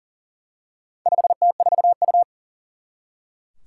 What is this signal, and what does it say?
Morse code, heard as a keyed tone, about 40 wpm, 700 Hz. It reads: 5T4U
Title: Four Challenging Letter-Number Combinations 40wpm